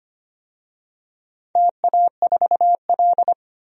Morse code, heard as a keyed tone, about 25 wpm, 700 Hz. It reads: TA4L
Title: DXCC-100 Call Signs - 4 Characters - 25wpm